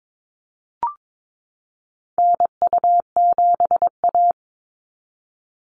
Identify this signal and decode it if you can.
Morse code, heard as a keyed tone, about 22 wpm, 700 Hz. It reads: NU7A
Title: NA SOTA Activator Callsigns 22wpm